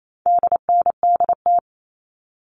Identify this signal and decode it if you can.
Morse code, heard as a keyed tone, about 28 wpm, 700 Hz. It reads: DNDT
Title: All Cut Numbers - 4-Digit 28wpm